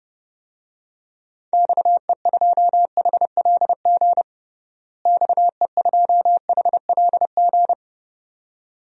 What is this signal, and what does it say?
Morse code, heard as a keyed tone, about 32 wpm, 700 Hz. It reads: XE2HLG XE2HLG
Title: DXCC-100 Call Signs - 6 Characters - Repeated 2x - 30wpm